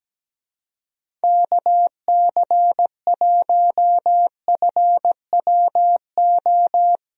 Morse code, heard as a keyed tone, about 17 wpm, 700 Hz. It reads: KC1FWO
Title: USA Call Signs - 6 Characters - Repeated 3x - 17wpm